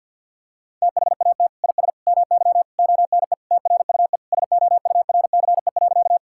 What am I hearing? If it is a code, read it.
Morse code, heard as a keyed tone, about 50 wpm, 700 Hz. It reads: THAT IS MY ONE TRUE SOURCE.
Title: Sentences from Top 1000 Words - Review Part 5 - 50wpm